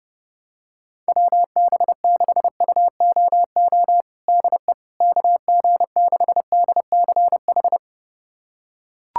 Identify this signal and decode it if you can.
Morse code, heard as a keyed tone, about 30 wpm, 700 Hz. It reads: WB6UOO DE KG6DCH
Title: USA Callsign DE Callsign 30wpm